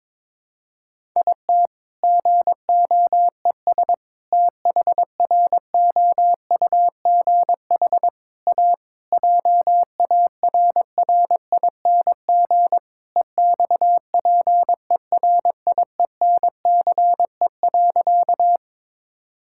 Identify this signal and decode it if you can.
Morse code, heard as a keyed tone, about 22 wpm, 700 Hz. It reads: IT GOES THROUGH A JARRING EXPERIENCE.
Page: morse-code-ninja-practice-22wpm